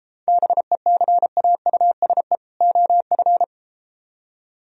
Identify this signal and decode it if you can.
Morse code, heard as a keyed tone, about 33 wpm, 700 Hz. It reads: BECAUSE OF